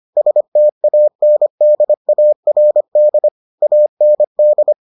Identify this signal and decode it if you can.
Morse code, heard as a keyed tone, about 25 wpm, 600 Hz. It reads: STANDARD AND